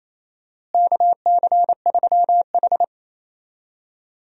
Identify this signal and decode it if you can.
Morse code, heard as a keyed tone, about 28 wpm, 700 Hz. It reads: KC3H
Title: USA Call Signs - 4 Characters - Repeated 2x - 28wpm